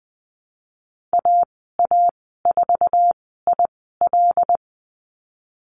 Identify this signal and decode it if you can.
Morse code, heard as a keyed tone, about 20 wpm, spaced as for 15 wpm, 700 Hz. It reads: AA4IL
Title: 082 Lesson 14g - Number 2 - Call Signs 20wpm